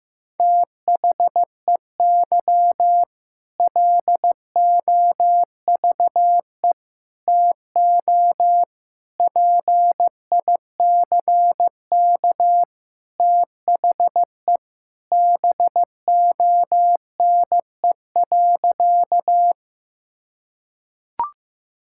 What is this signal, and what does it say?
Morse code, heard as a keyed tone, about 15 wpm, 700 Hz. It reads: THEY LOVE TO PICK THE BONE.
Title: Sentences from Top 1600 Words 15wpm